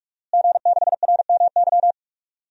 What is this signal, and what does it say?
Morse code, heard as a keyed tone, about 45 wpm, 700 Hz. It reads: G6RMY
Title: DXCC-100 Call Signs - 5 Characters - Repeated 2x - 45wpm